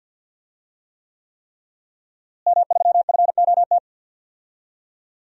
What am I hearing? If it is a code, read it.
Morse code, heard as a keyed tone, about 49 wpm, 700 Hz. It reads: M3FOT
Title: DXCC-100 Call Signs - 5 Characters - Repeated 3x - 50wpm